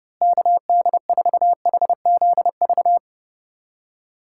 Morse code, heard as a keyed tone, about 32 wpm, 700 Hz. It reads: KD4HZV
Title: USA Callsign DE Callsign 30wpm